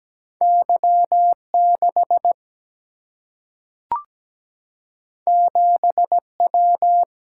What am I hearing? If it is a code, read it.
Morse code, heard as a keyed tone, about 17 wpm, 700 Hz. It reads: Y 6 7 W
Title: Single-Letter Single-Number 17wpm